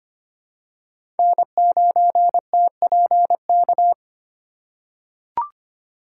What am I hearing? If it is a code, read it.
Morse code, heard as a keyed tone, about 25 wpm, 700 Hz. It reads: N9TPK
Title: USA Call Signs - 5 Characters - 3x Character Spacing - 25wpm